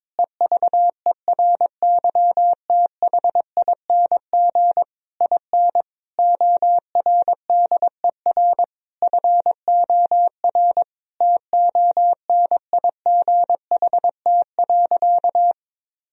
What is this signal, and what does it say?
Morse code, heard as a keyed tone, about 22 wpm, 700 Hz. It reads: EVERYTHING IN ORDER FOR TONIGHT.
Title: Sentences from Top 1000 Words - Review Part 3 - 22wpm